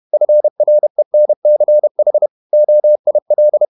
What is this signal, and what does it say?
Morse code, heard as a keyed tone, about 31 wpm, 600 Hz. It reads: FRENCH OIL
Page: Never 31wpm | CW med Gnister